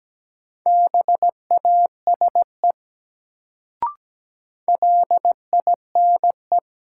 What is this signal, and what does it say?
Morse code, heard as a keyed tone, about 17 wpm, 700 Hz. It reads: BASE LINE
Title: Sentences from Top 1000 Words - Review Part 1 - New Word Drill - 17wpm